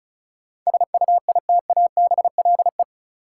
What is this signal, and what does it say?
Morse code, heard as a keyed tone, about 34 wpm, 700 Hz. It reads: SUITABLE
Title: Sentences from Top 1400 Words - New Word Drill Alt - 35wpm